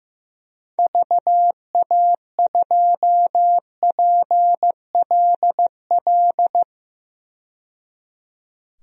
Morse code, heard as a keyed tone, about 15 wpm, 700 Hz. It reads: VA2PLL
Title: DXCC-100 Call Signs - 6 Characters - 15wpm